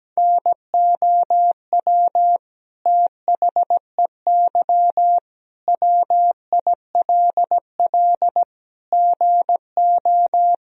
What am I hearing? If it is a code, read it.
Morse code, heard as a keyed tone, about 17 wpm, 700 Hz. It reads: NOW THEY WILL GO
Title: Sentences from Top 1000 Words - Review Part 4 - 17wpm